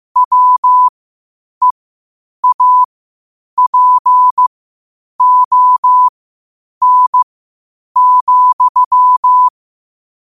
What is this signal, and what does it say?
Morse code, heard as a keyed tone, about 15 wpm, spaced as for 8 wpm, 1000 Hz. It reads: WEAPON,
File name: qotdfiles / 8wpm